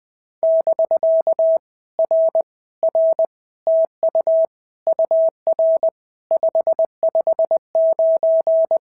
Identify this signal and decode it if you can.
Morse code, heard as a keyed tone, about 20 wpm, 650 Hz. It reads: <BK> R R TU UR 559
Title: POTA QSO Practice - 20wpm